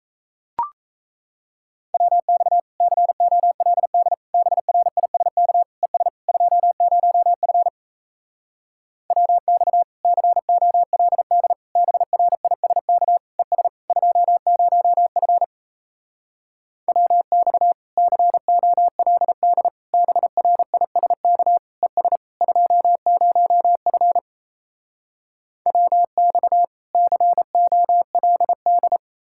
Morse code, heard as a keyed tone, about 38 wpm, 700 Hz. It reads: WX COLD BRISK ES 20F WX COLD BRISK ES 20F WX COLD BRISK ES 20F WX COLD